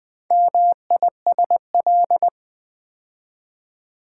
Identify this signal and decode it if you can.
Morse code, heard as a keyed tone, about 20 wpm, 700 Hz. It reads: MISL
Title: Build A Word - 4 Letters At A Time - 10-Letter Words - 20wpm